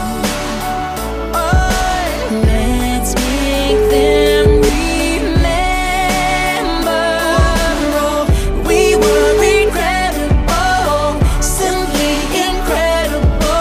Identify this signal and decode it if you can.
Ringback tone (Europe, 425 Hz), with noise.